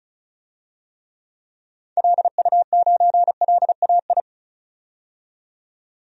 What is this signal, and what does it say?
Morse code, heard as a keyed tone, about 34 wpm, 700 Hz. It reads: LU9LAI
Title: DXCC-100 Call Signs - 6 Characters - 35wpm